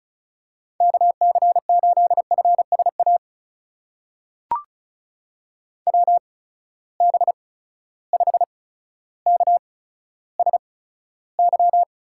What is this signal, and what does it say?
Morse code, heard as a keyed tone, about 36 wpm, spaced as for 15 wpm, 700 Hz. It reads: KC8FSA WB5KSY
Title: USA Call Signs - 8x Character Spacing - 35wpm